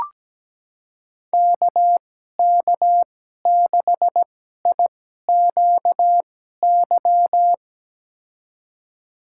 Morse code, heard as a keyed tone, about 17 wpm, spaced as for 12 wpm, 700 Hz. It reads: KK6IQY
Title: USA Call Signs - 6 Characters - 2x Character Spacing - 17wpm